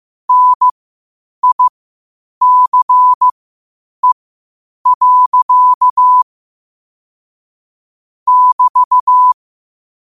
Morse code, heard as a keyed tone, about 15 wpm, spaced as for 8 wpm, 1000 Hz. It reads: NICE. <BT>
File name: qotdfiles / 8wpm